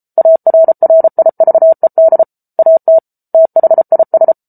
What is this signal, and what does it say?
Morse code, heard as a keyed tone, about 34 wpm, 650 Hz. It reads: ARRIVED AT THIS